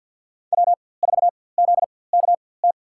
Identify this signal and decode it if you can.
Morse code, heard as a keyed tone, about 46 wpm, spaced as for 23 wpm, 700 Hz. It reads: W4ZKT
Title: USA Call Signs - 5 Characters - 4x Character Spacing - 50wpm